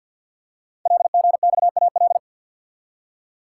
Morse code, heard as a keyed tone, about 50 wpm, 700 Hz. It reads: LGXAL
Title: Code Groups - 5 Letters - 50wpm